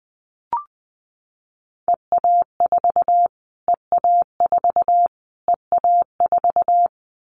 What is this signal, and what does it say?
Morse code, heard as a keyed tone, about 20 wpm, 700 Hz. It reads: EA4 EA4 EA4
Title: Callsign Prefixes for Top 25 DXCC - Repeated 3x - 20wpm